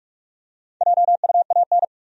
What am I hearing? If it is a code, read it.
Morse code, heard as a keyed tone, about 47 wpm, 700 Hz. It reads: JUAN
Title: USA Common Names - Max 4 Letters - 45wpm